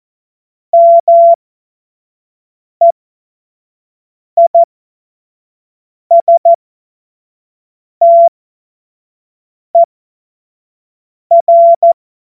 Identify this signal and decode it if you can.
Morse code, heard as a keyed tone, about 14 wpm, spaced as for 5 wpm, 680 Hz. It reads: MEISTER
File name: Morsekurs / Witze / Witz 15. November 2022 25 F70